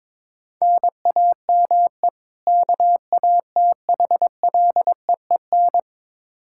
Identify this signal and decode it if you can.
Morse code, heard as a keyed tone, about 22 wpm, 700 Hz. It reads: NAME KATHLEEN
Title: Practice QSO - Exchange 1 - Beginning - 22wpm